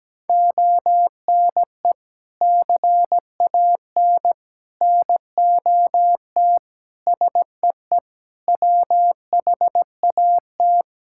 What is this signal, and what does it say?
Morse code, heard as a keyed tone, about 17 wpm, 700 Hz. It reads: ONE CAN NOT SEE WHAT
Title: Sentences from Top 1000 Words - Review Part 5 - 17wpm